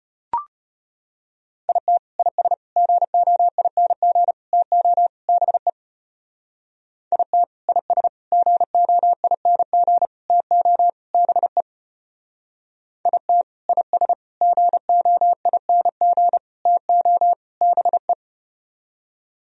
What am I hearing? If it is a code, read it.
Morse code, heard as a keyed tone, about 33 wpm, 700 Hz. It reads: IT IS GOING TO BE IT IS GOING TO BE IT IS GOING TO BE